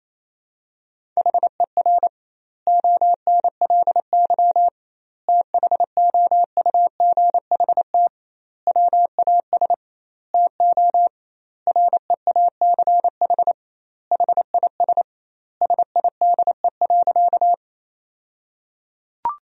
Morse code, heard as a keyed tone, about 28 wpm, 700 Hz. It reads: HER ONLY THOUGHT WAS TO REACH HIS SIDE.
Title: 2x Word-Spacing - Sentences from Top 400 Words - 28wpm